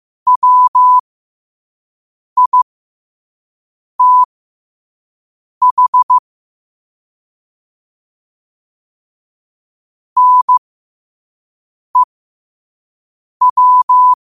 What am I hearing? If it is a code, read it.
Morse code, heard as a keyed tone, about 15 wpm, spaced as for 5 wpm, 1000 Hz. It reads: WITH NEW
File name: qotdfiles / 5wpm